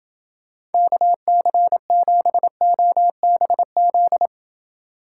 Morse code, heard as a keyed tone, about 28 wpm, 700 Hz. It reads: KC7OBZ